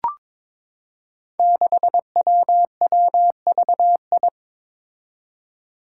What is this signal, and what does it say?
Morse code, heard as a keyed tone, about 22 wpm, 700 Hz. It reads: 6WWVI
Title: Five Challenging Letter-Number Combinations 22wpm